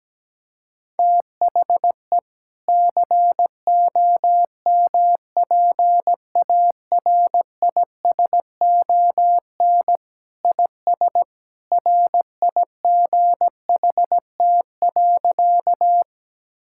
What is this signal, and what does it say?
Morse code, heard as a keyed tone, about 17 wpm, 700 Hz. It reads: THE COMPARISON IS RIGHT.
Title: Sentences from Top 1600 Words 17wpm